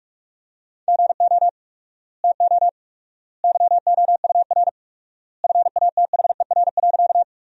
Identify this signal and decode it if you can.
Morse code, heard as a keyed tone, about 46 wpm, 700 Hz. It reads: GO TO YOUR FATHER.
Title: 4x Word-Spacing - Sentences from Top 300 Words - 45wpm